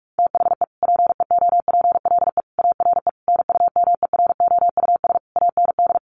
Morse code, heard as a keyed tone, about 44 wpm, 700 Hz. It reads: THE PEOPLE ARE NUMEROUS AND
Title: Sentences from Top 1400 Words 45wpm